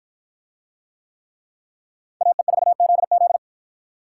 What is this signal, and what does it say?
Morse code, heard as a keyed tone, about 53 wpm, 700 Hz. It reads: AE4ZZ